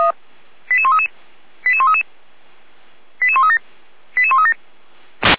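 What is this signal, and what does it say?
Five-tone selective call (ZVEI), heard as quick runs of five tones, tones of about 70 ms. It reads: 88133 88133 88137 88137
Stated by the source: ZVEI five-tone call 88402 received by SDR, from the fire brigade feuerwehr-orlamuende.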